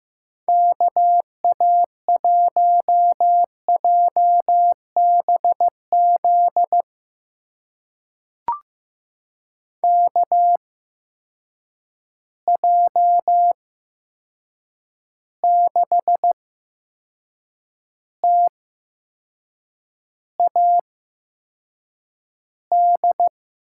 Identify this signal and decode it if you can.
Morse code, heard as a keyed tone, about 15 wpm, spaced as for 6 wpm, 700 Hz. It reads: KA1JBZ KJ6TAD
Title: USA Call Signs - 6 Characters - 8x Character Spacing - 15wpm